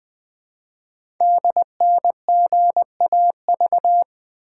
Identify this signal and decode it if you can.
Morse code, heard as a keyed tone, about 20 wpm, 700 Hz. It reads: DNGAV
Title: Code Groups - 5 Letters - 20wpm